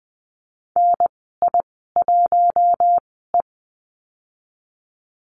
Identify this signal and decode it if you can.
Morse code, heard as a keyed tone, about 20 wpm, spaced as for 14 wpm, 700 Hz. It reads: NI1E
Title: 042 Lesson 8g - Number 1 - Call Signs 20wpm